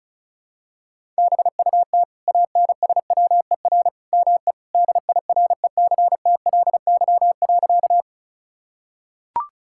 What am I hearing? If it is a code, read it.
Morse code, heard as a keyed tone, about 34 wpm, 700 Hz. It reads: BUT ANSWER ME DIRECTLY.
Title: Sentences from Top 1000 Words - Review Part 5 - 35wpm